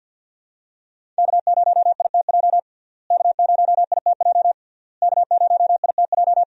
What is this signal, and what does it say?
Morse code, heard as a keyed tone, about 49 wpm, 700 Hz. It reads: K0ITJ K0ITJ K0ITJ
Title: USA Call Signs - 5 Characters - Repeated 3x - 50wpm